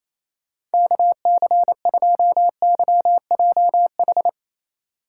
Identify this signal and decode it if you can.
Morse code, heard as a keyed tone, about 35 wpm, 700 Hz. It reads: KC2YJH